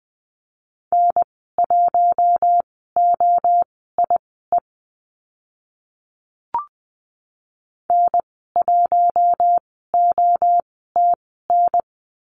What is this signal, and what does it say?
Morse code, heard as a keyed tone, about 20 wpm, spaced as for 14 wpm, 700 Hz. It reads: N1OIE N1OTN
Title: 042 Lesson 8g - Number 1 - Call Signs 20wpm